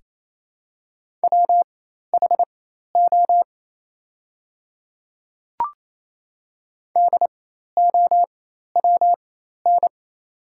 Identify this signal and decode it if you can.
Morse code, heard as a keyed tone, about 28 wpm, spaced as for 13 wpm, 700 Hz. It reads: WHO DOWN
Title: 4x Character Spacing - Top 200 Words - 28wpm